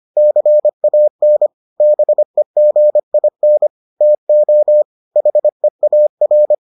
Morse code, heard as a keyed tone, about 25 wpm, 600 Hz. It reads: CAN BEGIN TO HEAR